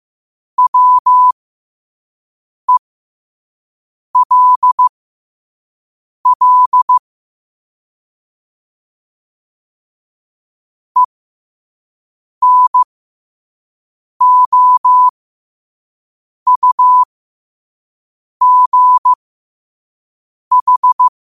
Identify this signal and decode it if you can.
Morse code, heard as a keyed tone, about 15 wpm, spaced as for 5 wpm, 1000 Hz. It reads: WELL ENOUGH
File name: qotdfiles / 5wpm